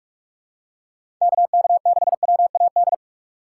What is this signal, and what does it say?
Morse code, heard as a keyed tone, about 43 wpm, 700 Hz. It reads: KK6WAD
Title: USA Call Signs - 6 Characters - 6x Character Spacing - 45wpm